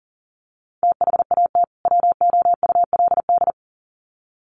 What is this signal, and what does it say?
Morse code, heard as a keyed tone, about 40 wpm, 700 Hz. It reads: THAT WOULD
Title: Sets of 2 Words 40wpm